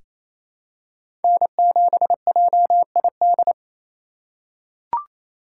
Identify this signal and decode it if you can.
Morse code, heard as a keyed tone, about 27 wpm, 700 Hz. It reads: N7JID
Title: USA Call Signs - 4x Character Spacing - 28wpm